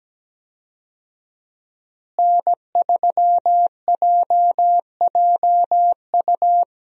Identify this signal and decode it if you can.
Morse code, heard as a keyed tone, about 17 wpm, 700 Hz. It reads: N3JJU
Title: USA Call Signs - 5 Characters - Repeated 2x - 17wpm